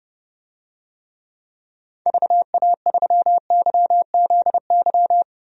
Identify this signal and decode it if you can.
Morse code, heard as a keyed tone, about 32 wpm, 700 Hz. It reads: VA3YZY
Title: DXCC-100 Call Signs - 6 Characters - Repeated 2x - 30wpm